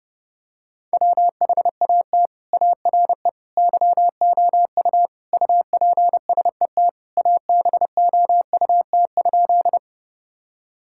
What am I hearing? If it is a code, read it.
Morse code, heard as a keyed tone, about 30 wpm, 700 Hz. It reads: WHAT ARE YOU UPSET ABOUT?
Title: Sentences from Top 1000 Words - Review Part 5 - 30wpm